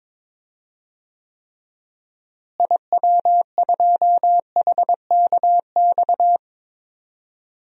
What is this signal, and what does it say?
Morse code, heard as a keyed tone, about 22 wpm, 700 Hz. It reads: IW2HKX
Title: DXCC-100 Call Signs - 6 Characters - Repeated 3x - 22wpm